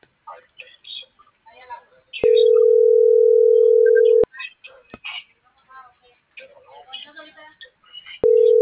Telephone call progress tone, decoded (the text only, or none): ringback tone